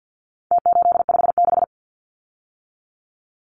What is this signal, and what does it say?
Morse code, heard as a keyed tone, about 49 wpm, 700 Hz. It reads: T856
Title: Numbers - Contest Style - 4-Digit 50wpm